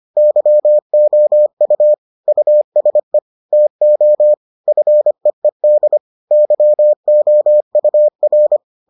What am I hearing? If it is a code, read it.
Morse code, heard as a keyed tone, about 25 wpm, 600 Hz. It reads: YOU USE TO FEED YOUR